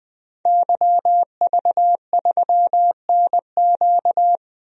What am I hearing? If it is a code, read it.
Morse code, heard as a keyed tone, about 20 wpm, 700 Hz. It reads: YV3NQ
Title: DXCC-100 Call Signs - 5 Characters - 20wpm